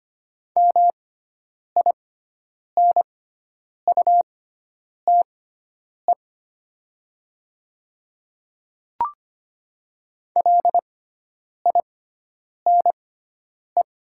Morse code, heard as a keyed tone, about 25 wpm, spaced as for 9 wpm, 690 Hz. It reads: MINUTE LINE
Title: 6x Character Spacing - Top 500 Words - 25wpm